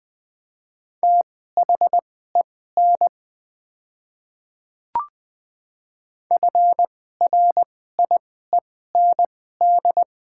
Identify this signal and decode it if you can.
Morse code, heard as a keyed tone, about 20 wpm, spaced as for 15 wpm, 700 Hz. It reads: THEN FRIEND
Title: 2x Character Spacing - Top 300 Words - 20wpm